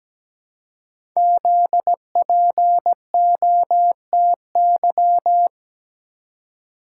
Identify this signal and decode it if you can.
Morse code, heard as a keyed tone, about 17 wpm, 700 Hz. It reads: ZPOTY
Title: Code Groups - 5 Letters - 17wpm